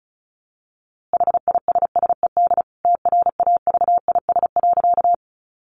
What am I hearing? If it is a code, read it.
Morse code, heard as a keyed tone, about 34 wpm, 700 Hz. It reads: HISSED TRAVIS.
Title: A Sound Of Thunder 35wpm